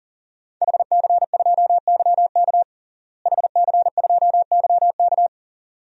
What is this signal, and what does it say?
Morse code, heard as a keyed tone, about 40 wpm, 700 Hz. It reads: HC2YK HC2YK
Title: DXCC-100 Call Signs - 5 Characters - Repeated 2x - 40wpm